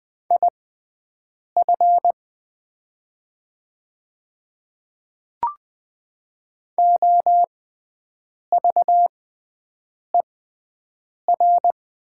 Morse code, heard as a keyed tone, about 20 wpm, spaced as for 7 wpm, 700 Hz. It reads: IF OVER
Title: 6x Character Spacing - Top 200 Words - 20wpm